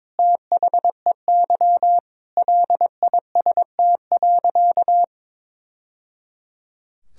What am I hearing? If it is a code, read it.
Morse code, heard as a keyed tone, about 22 wpm, 700 Hz. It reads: THEY LIST.
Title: Sentences from Top 1000 Words - Review Part 2 - 22wpm